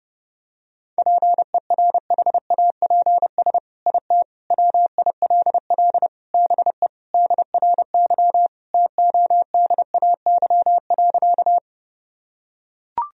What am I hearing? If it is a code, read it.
Morse code, heard as a keyed tone, about 30 wpm, 700 Hz. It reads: PERHAPS IT WILL BE DRY TODAY.
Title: Sentences from Top 1000 Words - Review Part 3 - 30wpm